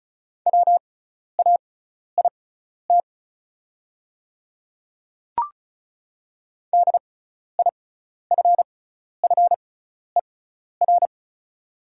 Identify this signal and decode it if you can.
Morse code, heard as a keyed tone, about 36 wpm, spaced as for 12 wpm, 700 Hz. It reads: WAIT DIFFER
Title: 6x Character Spacing - Top 500 Words - 35wpm